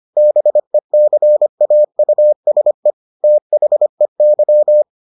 Morse code, heard as a keyed tone, about 25 wpm, 600 Hz. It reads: BECAUSE THEY